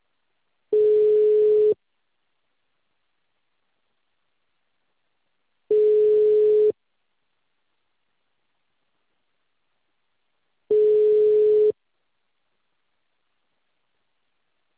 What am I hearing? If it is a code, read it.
Ringback tone (Europe, 425 Hz).
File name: out- -2044- - - .2541